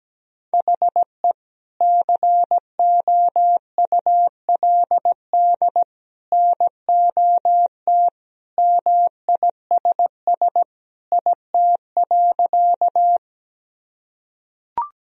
Morse code, heard as a keyed tone, about 17 wpm, 700 Hz. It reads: HE COULD NOT MISS IT.
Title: Sentences from Top 1000 Words - Review Part 3 - 17wpm